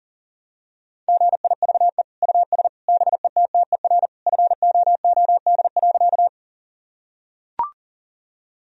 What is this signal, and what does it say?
Morse code, heard as a keyed tone, about 40 wpm, 700 Hz. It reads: GIVE US BETTER FOOD.
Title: Sentences from Top 1000 Words - Review Part 1 - 40wpm